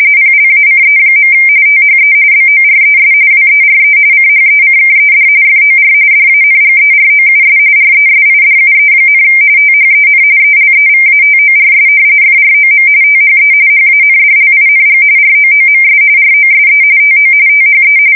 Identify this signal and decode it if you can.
RTTY, about 45.45 baud. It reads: PLACE. THE BORES WERE CHECKED FOR DANGEROUS RESIDUE, THEN ELEVATORS FROM THE HANDLING ROOMS ROSE TO THE
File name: fsk170 / battleshipnj